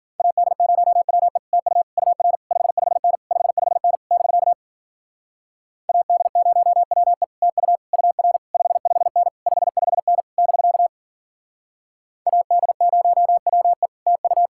Morse code, heard as a keyed tone, about 49 wpm, 700 Hz. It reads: AD0WE TU UR 55N 55N <BK> AD0WE TU UR 55N 55N <BK> AD0WE TU
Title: Speed-Racing Common QSO Phrases Encore - 40wpm